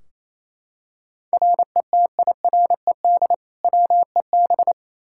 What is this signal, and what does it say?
Morse code, heard as a keyed tone, about 28 wpm, 700 Hz. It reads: RETIRED WEB
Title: Practice QSO - Exchange 3 - More Info - 28wpm